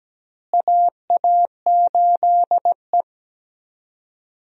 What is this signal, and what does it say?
Morse code, heard as a keyed tone, about 17 wpm, 700 Hz. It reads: AA8E
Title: USA Call Signs - 4 Characters - Repeated 2x - 17wpm